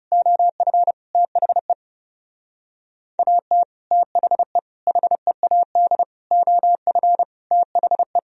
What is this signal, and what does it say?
Morse code, heard as a keyed tone, about 39 wpm, 700 Hz. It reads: OF THE AT THE HEAD OF THE